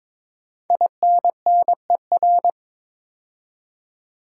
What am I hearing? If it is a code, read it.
Morse code, heard as a keyed tone, about 22 wpm, 700 Hz. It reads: INNER
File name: 5 Letter Words - First 3 Letters Given - 22wpm